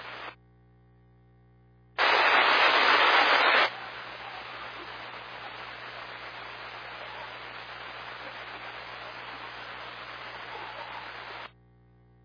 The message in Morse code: none

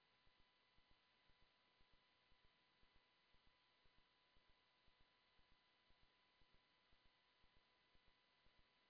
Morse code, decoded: none

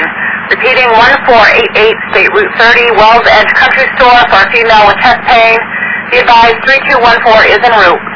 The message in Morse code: none

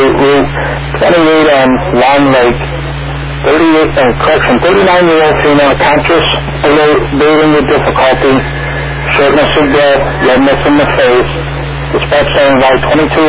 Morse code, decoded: none

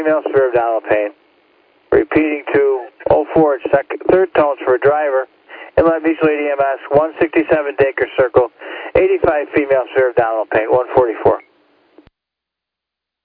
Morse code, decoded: none

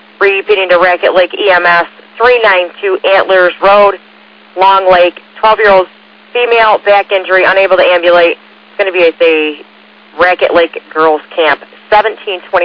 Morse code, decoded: none

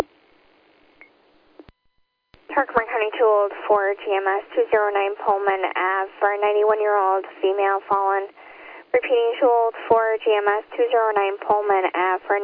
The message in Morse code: none

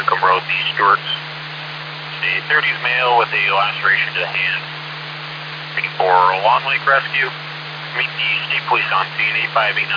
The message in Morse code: none

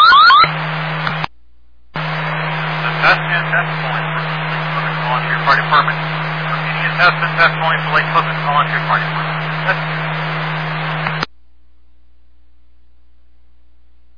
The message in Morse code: none

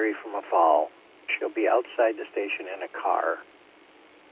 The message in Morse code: none

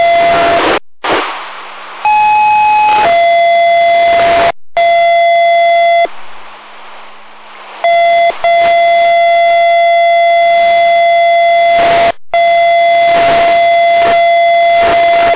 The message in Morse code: none

